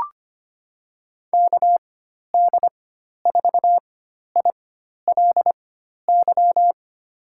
Morse code, heard as a keyed tone, about 25 wpm, spaced as for 12 wpm, 700 Hz. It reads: KD4ILY
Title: USA Call Signs - 6 Characters - 4x Character Spacing - 25wpm